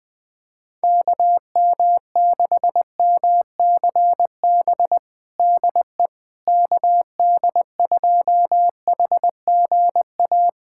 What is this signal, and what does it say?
Morse code, heard as a keyed tone, about 20 wpm, 700 Hz. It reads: KM6MCB DE KD2HGA
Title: USA Callsign DE Callsign 20wpm